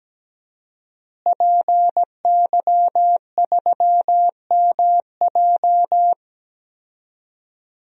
Morse code, heard as a keyed tone, about 17 wpm, 700 Hz. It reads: PY3MJ
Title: DXCC-100 Call Signs - 5 Characters - Repeated 2x - 17wpm